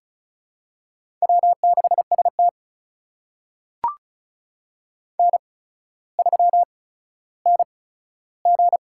Morse code, heard as a keyed tone, about 36 wpm, spaced as for 15 wpm, 700 Hz. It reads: W6ST N3NG
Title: USA Call Signs - 4 Characters - 8x Character Spacing - 35wpm